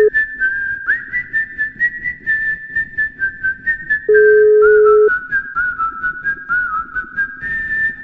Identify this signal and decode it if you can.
Ringback tone (Europe, 425 Hz).